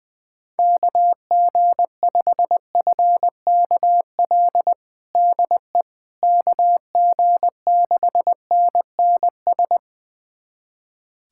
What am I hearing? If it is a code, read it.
Morse code, heard as a keyed tone, about 21 wpm, 700 Hz. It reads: KG5FKL DE KG6NNS
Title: USA Callsign DE Callsign 20wpm